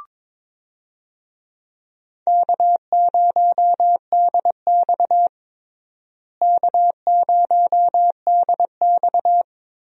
Morse code, heard as a keyed tone, about 21 wpm, 700 Hz. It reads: K0DX K0DX
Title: USA Call Signs - 4 Characters - Repeated 2x - 22wpm